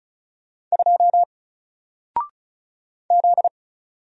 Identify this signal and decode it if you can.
Morse code, heard as a keyed tone, about 36 wpm, 700 Hz. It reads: 2 Z
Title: Single Character (Letter:Number:Punctuation:Prosign) 35wpm